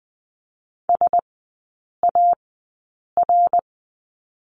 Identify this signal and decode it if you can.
Morse code, heard as a keyed tone, about 20 wpm, spaced as for 8 wpm, 690 Hz. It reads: SAR
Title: 093 Lesson 16e - Letter c - Three Characters 20wpm